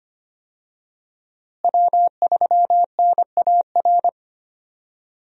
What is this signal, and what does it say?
Morse code, heard as a keyed tone, about 25 wpm, 700 Hz. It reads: W3NAR
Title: USA Call Signs - 5 Characters - Repeated 3x - 25wpm